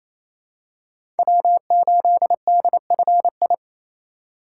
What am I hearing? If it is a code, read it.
Morse code, heard as a keyed tone, about 28 wpm, 700 Hz. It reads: W8DFI
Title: USA Call Signs - 3x Character Spacing - 28wpm